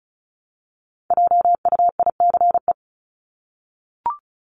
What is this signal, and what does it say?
Morse code, heard as a keyed tone, about 34 wpm, 700 Hz. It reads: JUICE
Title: Top 1251-1500 Words 35wpm